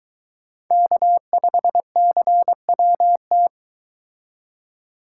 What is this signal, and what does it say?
Morse code, heard as a keyed tone, about 22 wpm, 700 Hz. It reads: K5CWT